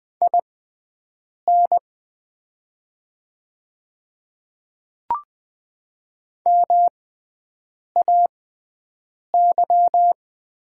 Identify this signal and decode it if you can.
Morse code, heard as a keyed tone, about 20 wpm, spaced as for 7 wpm, 690 Hz. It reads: IN MAY
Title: 6x Character Spacing - Top 100 Words - 20wpm